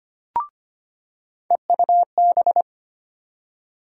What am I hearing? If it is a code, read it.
Morse code, heard as a keyed tone, about 25 wpm, 700 Hz. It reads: EUB
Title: USA Call Sign Postfixes - 3 Letters 25wpm